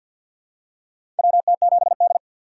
Morse code, heard as a keyed tone, about 49 wpm, 700 Hz. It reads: WT7D
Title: USA Call Signs - 4 Characters - 2x Character Spacing - 50wpm